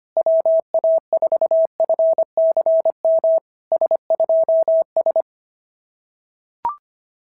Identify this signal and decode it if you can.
Morse code, heard as a keyed tone, about 24 wpm, 650 Hz. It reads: WA4FCM S2S
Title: SOTA QSO Practice 25wpm